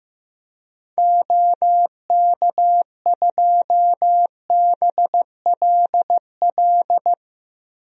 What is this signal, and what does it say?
Morse code, heard as a keyed tone, about 16 wpm, 700 Hz. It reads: OK2BLL
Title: DXCC-100 Call Signs - 6 Characters - Repeated 3x - 15wpm